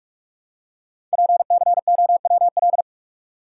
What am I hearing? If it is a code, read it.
Morse code, heard as a keyed tone, about 44 wpm, 700 Hz. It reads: PCOWL
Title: Code Groups - 5 Letters - 45wpm